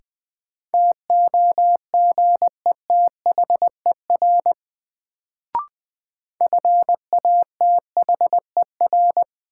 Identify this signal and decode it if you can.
Morse code, heard as a keyed tone, about 20 wpm, 700 Hz. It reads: TOGETHER FATHER
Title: Sentences from Top 300 Words - New Word Drill - 20wpm